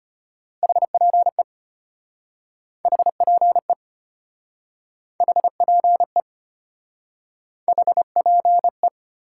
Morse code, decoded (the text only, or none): HPE HPE HPE HPE